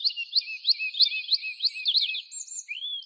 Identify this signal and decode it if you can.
There is no code.